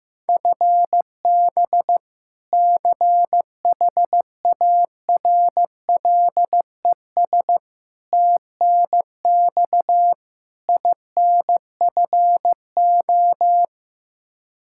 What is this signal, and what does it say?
Morse code, heard as a keyed tone, about 15 wpm, 700 Hz. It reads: FB CHARLES TNX INFO
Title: Practice QSO - Exchange 3 - More Info - 15wpm